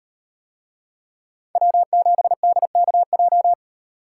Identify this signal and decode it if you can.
Morse code, heard as a keyed tone, about 38 wpm, 700 Hz. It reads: W7DKJ